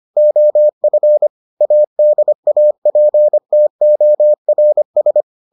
Morse code, heard as a keyed tone, about 25 wpm, 600 Hz. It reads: OF ADAPTORS